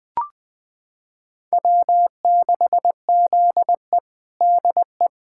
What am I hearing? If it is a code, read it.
Morse code, heard as a keyed tone, about 20 wpm, 700 Hz. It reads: W6ZE DE
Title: USA Callsign DE Callsign - 4 Characters - 20wpm